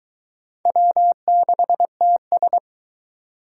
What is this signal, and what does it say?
Morse code, heard as a keyed tone, about 22 wpm, 700 Hz. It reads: W6TS